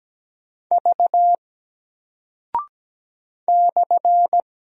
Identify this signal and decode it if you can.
Morse code, heard as a keyed tone, about 17 wpm, 700 Hz. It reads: V /
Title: Single Character (Letter:Number:Punctuation:Prosign) 17wpm